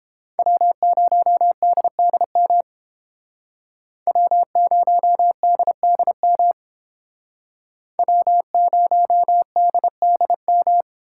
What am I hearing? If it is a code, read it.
Morse code, heard as a keyed tone, about 30 wpm, 700 Hz. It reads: W0DDM W0DDM W0DDM